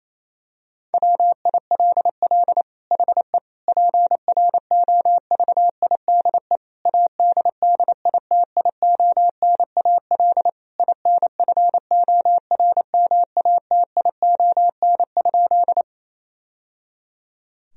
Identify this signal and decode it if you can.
Morse code, heard as a keyed tone, about 27 wpm, 700 Hz. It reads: WILL HE PROVIDE ADDITIONAL INFORMATION?
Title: Sentences from Top 1000 Words - Review Part 4 - 28wpm